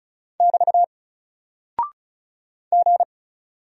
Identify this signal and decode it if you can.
Morse code, heard as a keyed tone, about 36 wpm, 700 Hz. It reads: <BT> G
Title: Single Character (Letter:Number:Punctuation:Prosign) 35wpm